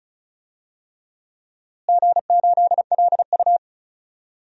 Morse code, heard as a keyed tone, about 35 wpm, 700 Hz. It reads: G8LU
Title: DXCC-100 Call Signs - 4 Characters - Repeated 3x - 35wpm